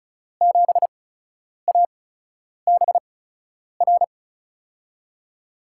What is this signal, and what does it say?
Morse code, heard as a keyed tone, about 36 wpm, spaced as for 10 wpm, 700 Hz. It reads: 7ABR
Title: USA Call Signs - 8x Character Spacing - 35wpm